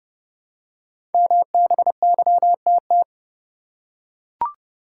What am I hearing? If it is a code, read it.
Morse code, heard as a keyed tone, about 30 wpm, 700 Hz. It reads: MBYTT
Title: Code Groups - 5 Letter:Numbers - 30wpm